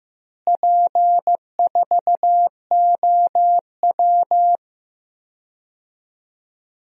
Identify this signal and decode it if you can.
Morse code, heard as a keyed tone, about 15 wpm, 700 Hz. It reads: P4OW
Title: DXCC-100 Call Signs - 4 Characters - Repeated 2x - 15wpm